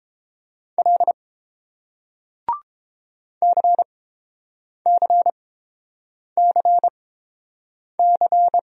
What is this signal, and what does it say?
Morse code, heard as a keyed tone, about 31 wpm, spaced as for 7 wpm, 700 Hz. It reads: L CCCC